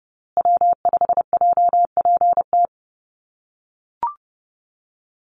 Morse code, heard as a keyed tone, about 30 wpm, 700 Hz. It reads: W5JPT
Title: USA Call Signs Repeated 2x 30wpm